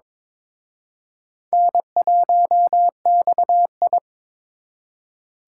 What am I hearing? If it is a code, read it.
Morse code, heard as a keyed tone, about 26 wpm, 700 Hz. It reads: N1XI
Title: Speed Racing - USA Call Signs 22wpm